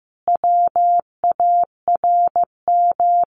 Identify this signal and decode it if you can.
Morse code, heard as a keyed tone, about 15 wpm, 700 Hz. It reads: WARM
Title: A Sound Of Thunder 15wpm